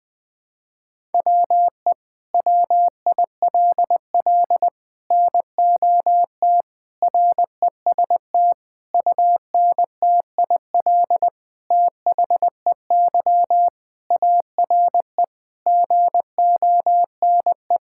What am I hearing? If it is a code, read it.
Morse code, heard as a keyed tone, about 20 wpm, 700 Hz. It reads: WE WILL NOT REST UNTIL THEY ARE GONE
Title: Sentences from Top 1600 Words 20wpm